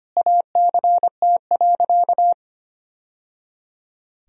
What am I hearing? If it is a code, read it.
Morse code, heard as a keyed tone, about 25 wpm, 700 Hz. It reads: ACT.
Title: Sentences from Top 1000 Words - Review Part 1 - 25wpm